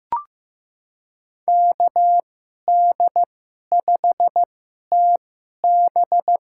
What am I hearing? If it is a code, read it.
Morse code, heard as a keyed tone, about 15 wpm, spaced as for 11 wpm, 700 Hz. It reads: KD5TB
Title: USA Call Signs - 5 Characters - 2x Character Spacing - 15wpm